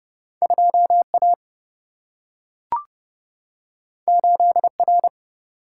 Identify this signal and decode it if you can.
Morse code, heard as a keyed tone, about 30 wpm, 700 Hz. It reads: 2 A 8 R
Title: Single-Letter Single-Number 30wpm